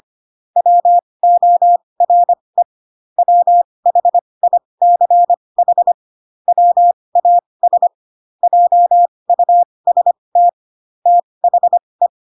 Morse code, heard as a keyed tone, about 25 wpm, spaced as for 20 wpm, 700 Hz. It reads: WORE WHICH WAS JUST THE
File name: Pinocchio / Pinocchio ch2 15 25 ik1hge